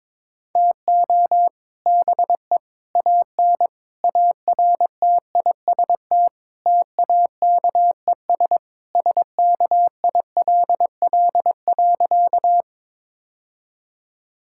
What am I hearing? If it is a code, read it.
Morse code, heard as a keyed tone, about 22 wpm, 700 Hz. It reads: TO BE AN ARTIST TAKES SKILL.
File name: practice / 22wpm / Sentences with 3-Word Phrases - 22wpm